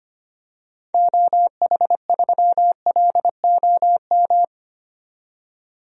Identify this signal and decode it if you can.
Morse code, heard as a keyed tone, about 25 wpm, 700 Hz. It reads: OH3LOM
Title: DXCC-100 Call Signs - 6 Characters - 25wpm